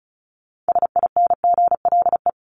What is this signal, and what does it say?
Morse code, heard as a keyed tone, about 34 wpm, 700 Hz. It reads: SINGLE
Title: Top 501-750 Words 35wpm